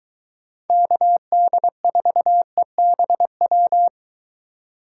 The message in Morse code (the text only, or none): KD4EBW